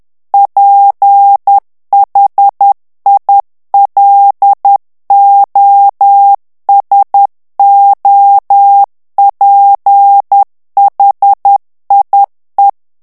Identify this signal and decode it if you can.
Morse code, heard as a keyed tone, about 10 wpm, 800 Hz. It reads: PHILOSOPHIE
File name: CW / FICHIERS-MP3 / FRANCAIS TEXTE CLAIR 2 VITESSE 10 MOTS MINUTES